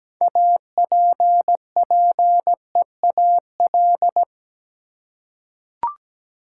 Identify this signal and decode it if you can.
Morse code, heard as a keyed tone, about 17 wpm, 700 Hz. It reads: APPEAL
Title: Six Letter Words - 17wpm